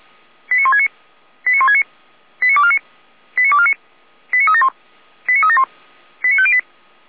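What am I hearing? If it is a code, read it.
Five-tone selective call (ZVEI), heard as quick runs of five tones, tones of about 70 ms. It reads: 89169 89169 89249 89249 89371 89371 89508
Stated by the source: ZVEI five-tone selective call received on SDR.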